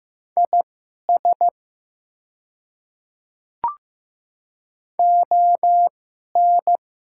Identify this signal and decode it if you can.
Morse code, heard as a keyed tone, about 15 wpm, spaced as for 11 wpm, 690 Hz. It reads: IS ON
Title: 2x Character Spacing - Top 100 Words - 15wpm